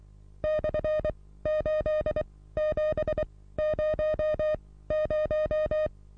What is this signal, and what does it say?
Morse code, heard as a keyed tone, about 23 wpm, spaced as for 16 wpm, 620 Hz. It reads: /8700
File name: repos / Morsekurs / DJ4UF Morsen27